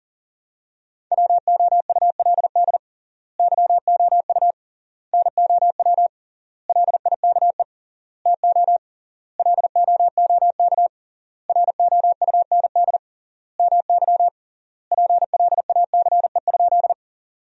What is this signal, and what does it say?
Morse code, heard as a keyed tone, about 40 wpm, 700 Hz. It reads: WOULD YOU NOW LIKE TO LOOK ROUND MY PLACE?
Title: 3x Word-Spacing - Sentences from Top 200 Words - 40wpm